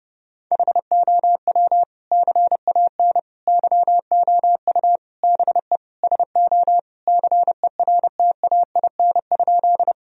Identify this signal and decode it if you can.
Morse code, heard as a keyed tone, about 30 wpm, 700 Hz. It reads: HOW CAN YOU BE SO CERTAIN?
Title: Sentences from Top 1000 Words - Review Part 2 - 30wpm